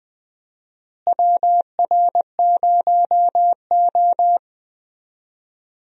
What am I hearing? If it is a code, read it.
Morse code, heard as a keyed tone, about 21 wpm, 700 Hz. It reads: WR0O
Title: USA Call Signs - 4 Characters - Repeated 3x - 20wpm